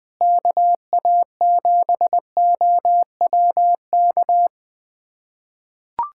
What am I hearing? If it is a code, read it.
Morse code, heard as a keyed tone, about 20 wpm, 700 Hz. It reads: KA7OWK
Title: USA Callsign DE Callsign 20wpm